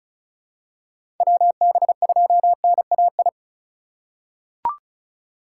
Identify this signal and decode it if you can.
Morse code, heard as a keyed tone, about 36 wpm, 700 Hz. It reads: WB2NAI
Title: USA Call Signs - 6 Characters - 8x Character Spacing - 35wpm